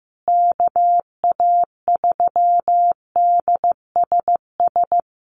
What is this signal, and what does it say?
Morse code, heard as a keyed tone, about 15 wpm, 700 Hz. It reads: KA3DSS
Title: USA Call Signs Repeated 3x 15wpm